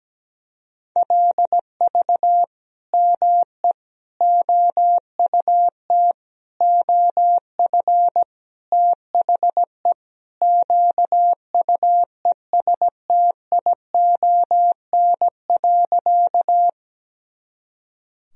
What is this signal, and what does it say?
Morse code, heard as a keyed tone, about 17 wpm, 700 Hz. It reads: LV ME OUT OF THE QUESTION.
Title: Sentences from Top 400 Words and Abbreviations 17wpm